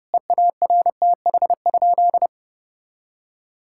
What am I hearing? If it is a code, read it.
Morse code, heard as a keyed tone, about 30 wpm, 700 Hz. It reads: EARTH?
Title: Sentences from Top 1000 Words - Review Part 5 - 30wpm